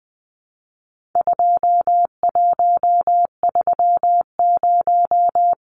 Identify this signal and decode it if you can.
Morse code, heard as a keyed tone, about 19 wpm, 700 Hz. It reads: 2130
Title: Numbers - 4-Digit 20wpm